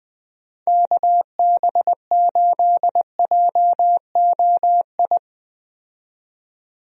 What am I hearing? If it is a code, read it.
Morse code, heard as a keyed tone, about 20 wpm, 700 Hz. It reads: KB8JOI
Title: USA Call Signs - 6 Characters - 20wpm